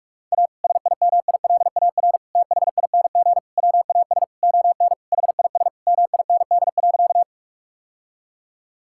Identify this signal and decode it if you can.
Morse code, heard as a keyed tone, about 46 wpm, 700 Hz. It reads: A SIMILAR THING WAS ON HIS MIND.
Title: Sentences from Top 1000 Words - Review Part 5 - 45wpm